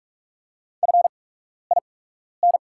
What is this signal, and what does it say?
Morse code, heard as a keyed tone, about 45 wpm, spaced as for 12 wpm, 700 Hz. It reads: FIN
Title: 8x Character Spacing - Top 100 Words - 45wpm